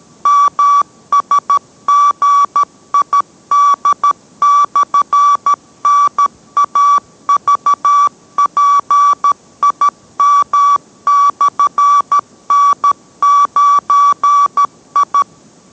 Morse code, heard as a keyed tone, about 13 wpm, 1150 Hz. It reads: MSGID/NAVPIM/N9I